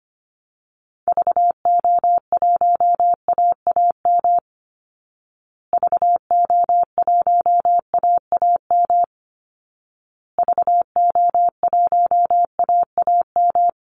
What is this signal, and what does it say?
Morse code, heard as a keyed tone, about 25 wpm, 700 Hz. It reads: VO1AAM VO1AAM VO1AAM
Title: Call Signs Repeated 2x - 25wpm